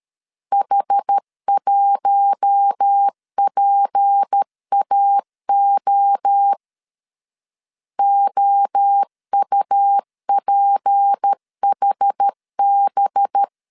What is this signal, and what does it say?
Morse code, heard as a keyed tone, about 13 wpm, 800 Hz. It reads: H1PAO OUPHB
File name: CW / DET / 1 6 1500C S B 12wpm 3431